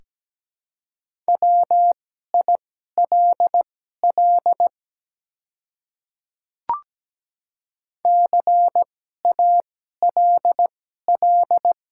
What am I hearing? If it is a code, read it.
Morse code, heard as a keyed tone, about 17 wpm, spaced as for 12 wpm, 700 Hz. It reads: WILL CALL
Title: 2x Character Spacing - Top 100 Words - 17wpm